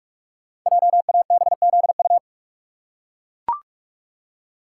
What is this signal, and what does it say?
Morse code, heard as a keyed tone, about 44 wpm, 700 Hz. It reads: JABZU
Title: Code Groups - 5 Letters - 45wpm